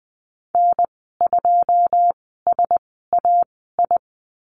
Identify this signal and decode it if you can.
Morse code, heard as a keyed tone, about 20 wpm, spaced as for 14 wpm, 700 Hz. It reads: N2SAI
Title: 128 Lesson 21g - Number 6 - Call Signs 20wpm